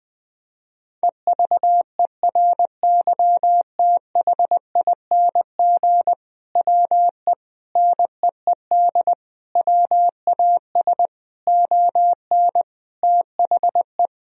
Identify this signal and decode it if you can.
Morse code, heard as a keyed tone, about 20 wpm, 700 Hz. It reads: EVERYTHING WE NEED WAS ON THE
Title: Sentences from Top 1000 Words - Review Part 1 - 20wpm